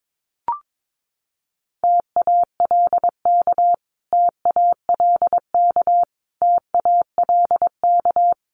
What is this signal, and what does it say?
Morse code, heard as a keyed tone, about 22 wpm, 700 Hz. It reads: TALK TALK TALK
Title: Top 500 Words Repeated 22wpm